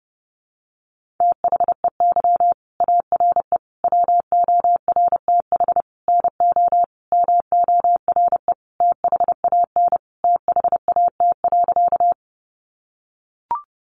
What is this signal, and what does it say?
Morse code, heard as a keyed tone, about 30 wpm, 700 Hz. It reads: THEY ARE WORTH NO MORE THAN THAT.
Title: Sentences from Top 800 Words 30wpm